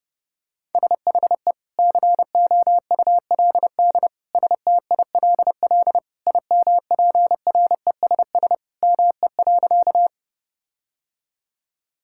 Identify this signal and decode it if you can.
Morse code, heard as a keyed tone, about 32 wpm, 700 Hz. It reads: SHE COULD STILL IMPRESS ME.
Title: Sentences from Top 1600 Words 30wpm